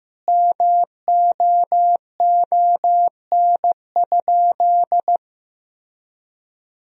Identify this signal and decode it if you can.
Morse code, heard as a keyed tone, about 16 wpm, 700 Hz. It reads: MOON?
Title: Sentences from Top 1000 Words - Review Part 3 - 15wpm